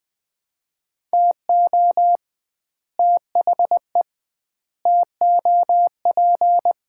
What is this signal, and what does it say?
Morse code, heard as a keyed tone, about 21 wpm, 700 Hz. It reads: TO THE TOP
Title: 2x Word-Spacing - Sets of 4 Words - 20wpm